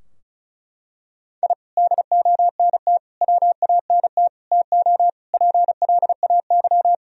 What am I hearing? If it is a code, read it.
Morse code, heard as a keyed tone, about 34 wpm, 700 Hz. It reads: I DONT WANT TO PLAY
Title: Sentences from Top 1000 Words - Review Part 2 - 35wpm